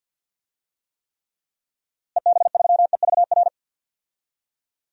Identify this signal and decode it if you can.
Morse code, heard as a keyed tone, about 53 wpm, 700 Hz. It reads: EB3EVR
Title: DXCC-100 Call Signs - 6 Characters - Repeated 3x - 50wpm